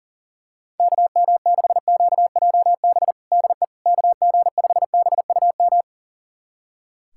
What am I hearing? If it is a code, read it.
Morse code, heard as a keyed tone, about 40 wpm, 700 Hz. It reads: KM6QJB DE KG5BUM
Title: USA Callsign DE Callsign - 6 Characters - 40wpm